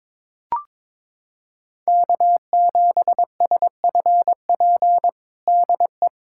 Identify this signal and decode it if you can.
Morse code, heard as a keyed tone, about 22 wpm, 700 Hz. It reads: K7SFP DE
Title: USA Callsign DE Callsign 22wpm